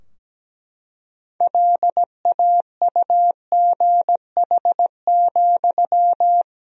Morse code, heard as a keyed tone, about 17 wpm, 700 Hz. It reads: LAUGH,
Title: 3x Word-Spacing - Sentences from Top 500 Words - 17wpm